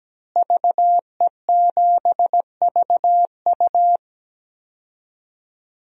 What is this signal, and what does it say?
Morse code, heard as a keyed tone, about 17 wpm, 700 Hz. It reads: VE7VU
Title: DXCC-100 Call Signs - 5 Characters - Repeated 3x - 17wpm